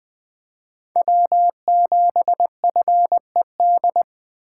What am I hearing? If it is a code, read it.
Morse code, heard as a keyed tone, about 20 wpm, 700 Hz. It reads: W7FED
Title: Speed Racing - USA 1x3 Call Signs 17wpm